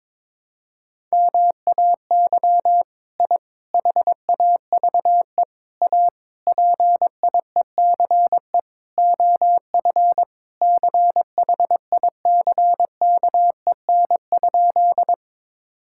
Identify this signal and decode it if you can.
Morse code, heard as a keyed tone, about 22 wpm, 700 Hz. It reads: MAY I HAVE A PIECE OF CHICKEN?
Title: Sentences from Top 1000 Words - Review Part 5 - 22wpm